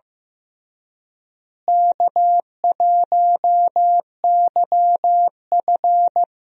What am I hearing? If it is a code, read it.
Morse code, heard as a keyed tone, about 16 wpm, 700 Hz. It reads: K1YF
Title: USA Call Signs - 4 Characters - Repeated 3x - 15wpm